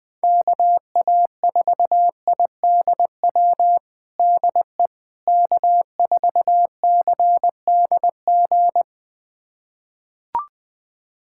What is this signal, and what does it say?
Morse code, heard as a keyed tone, about 20 wpm, 700 Hz. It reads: KA4IDW DE K4CDG
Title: USA Callsign DE Callsign 20wpm